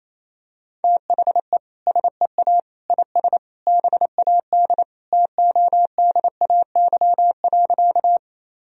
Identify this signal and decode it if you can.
Morse code, heard as a keyed tone, about 28 wpm, 700 Hz. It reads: THE SEA IS BAD TODAY.
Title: Sentences from Top 1000 Words - Review Part 1 - 28wpm